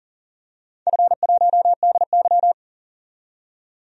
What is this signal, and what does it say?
Morse code, heard as a keyed tone, about 40 wpm, 700 Hz. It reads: F1DY
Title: DXCC-100 Call Signs - 4 Characters - 40wpm